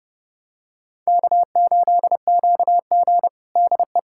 Morse code, heard as a keyed tone, about 30 wpm, 700 Hz. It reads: K8QG DE
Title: USA Callsign DE Callsign - 4 Characters - 30wpm